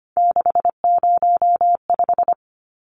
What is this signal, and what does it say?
Morse code, heard as a keyed tone, about 25 wpm, 700 Hz. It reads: 605
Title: Numbers - 3-Digit 25wpm